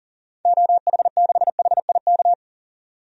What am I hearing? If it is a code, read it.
Morse code, heard as a keyed tone, about 40 wpm, 700 Hz. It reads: OH6HIK
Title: DXCC-100 Call Signs - 6 Characters - 40wpm